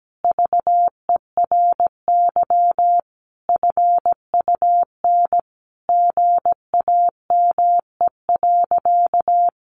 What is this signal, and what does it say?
Morse code, heard as a keyed tone, about 17 wpm, 700 Hz. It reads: VERY FUN GAME.
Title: Additional Easy Phrases 17wpm